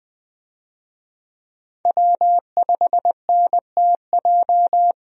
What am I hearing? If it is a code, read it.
Morse code, heard as a keyed tone, about 20 wpm, 700 Hz. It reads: W5NTJ
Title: USA Call Signs - 5 Characters - Repeated 3x - 20wpm